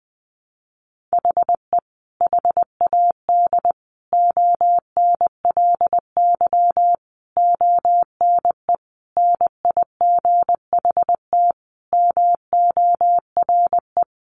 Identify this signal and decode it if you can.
Morse code, heard as a keyed tone, about 20 wpm, 700 Hz. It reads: HE HAD ONLY ONE NIGHT MORE
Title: Sentences from Top 300 Words 20wpm